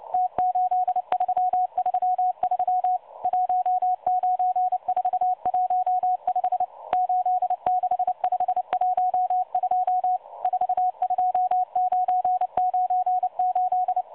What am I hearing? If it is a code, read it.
Morse code, heard as a keyed tone, about 29 wpm, 730 Hz. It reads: T8333 19415 86512 42998
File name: M12 - 8116 kHz CW 2-11-14